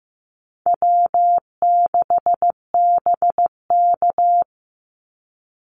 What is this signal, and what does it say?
Morse code, heard as a keyed tone, about 15 wpm, 700 Hz. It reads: W6BK
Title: Easy Call Signs - 15wpm